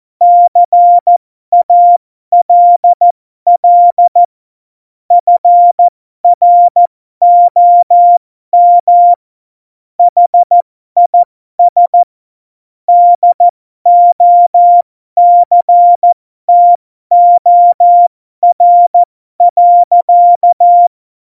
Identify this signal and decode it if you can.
Morse code, heard as a keyed tone, about 14 wpm, spaced as for 12 wpm, 700 Hz. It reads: CALL FROM HIS DOCTOR.
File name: Morsekurs / Witze